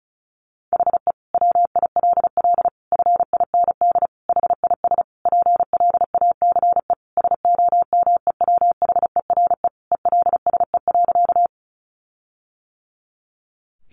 Morse code, heard as a keyed tone, about 34 wpm, 700 Hz. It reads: HE WILL FIND HIS PLACE SOMEWHERE ELSE.
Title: Sentences from Top 1400 Words 35wpm